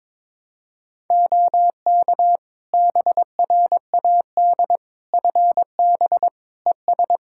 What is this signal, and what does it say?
Morse code, heard as a keyed tone, about 22 wpm, 700 Hz. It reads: OK BRAD FB ES
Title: K1USN SST QSO Practice 22wpm